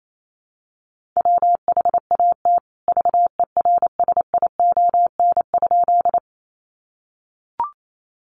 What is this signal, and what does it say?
Morse code, heard as a keyed tone, about 28 wpm, 700 Hz. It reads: WHAT VERSION?
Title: Sentences from Top 1400 Words 28wpm